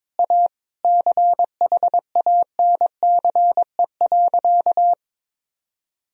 Morse code, heard as a keyed tone, about 22 wpm, 700 Hz. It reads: A CHANCE.
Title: Sentences from Top 1000 Words - Review Part 2 - 22wpm